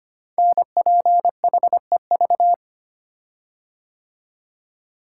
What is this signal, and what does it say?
Morse code, heard as a keyed tone, about 25 wpm, 700 Hz. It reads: NPHEV
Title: Code Groups - 5 Letters - 25wpm